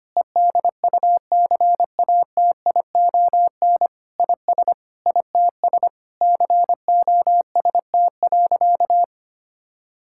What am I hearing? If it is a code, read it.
Morse code, heard as a keyed tone, about 25 wpm, 700 Hz. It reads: EDUCATION IS ITS COST.
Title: Sentences from Top 1000 Words - Review Part 6 - 25wpm